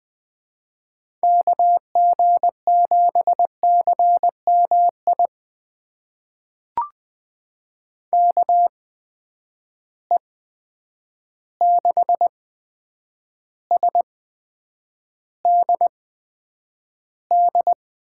Morse code, heard as a keyed tone, about 20 wpm, spaced as for 9 wpm, 700 Hz. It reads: KG7CMI KE6SDD
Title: USA Call Signs - 6 Characters - 8x Character Spacing - 20wpm